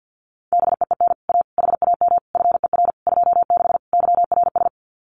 Morse code, heard as a keyed tone, about 48 wpm, 700 Hz. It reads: BEEN A HAM FER 26 YRS
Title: Common QSO Phrases Encore 50wpm